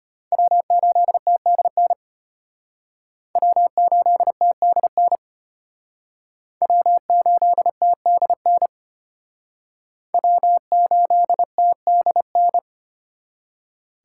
Morse code, heard as a keyed tone, about 33 wpm, 700 Hz. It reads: W8TDN W8TDN W8TDN W8TDN